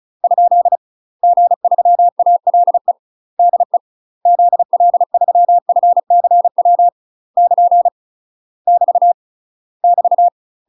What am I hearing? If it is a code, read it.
Morse code, heard as a keyed tone, about 36 wpm, 700 Hz. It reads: ? G3ALE DE ZL3FCW <KN> <BT> <BT>
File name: UK MASTER QSOs TRAINING / 35WPM / UK MASTER QSOs 0017